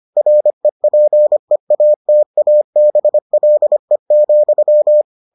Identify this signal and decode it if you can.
Morse code, heard as a keyed tone, about 25 wpm, 600 Hz. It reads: REPEATABLE,